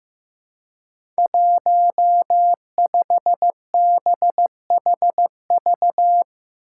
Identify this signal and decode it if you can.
Morse code, heard as a keyed tone, about 15 wpm, 700 Hz. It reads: 15BHV
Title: Five Challenging Letter-Number Combinations 15wpm